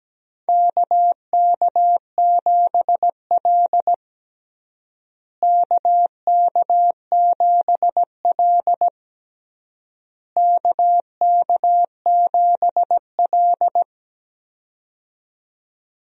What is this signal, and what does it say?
Morse code, heard as a keyed tone, about 17 wpm, 700 Hz. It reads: KK7L KK7L KK7L
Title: USA Call Signs - 4 Characters - Repeated 3x - 17wpm